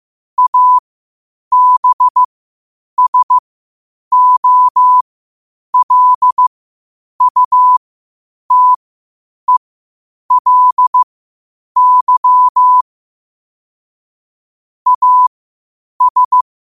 Morse code, heard as a keyed tone, about 15 wpm, spaced as for 8 wpm, 1000 Hz. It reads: ABSOLUTELY AS
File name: qotdfiles / 8wpm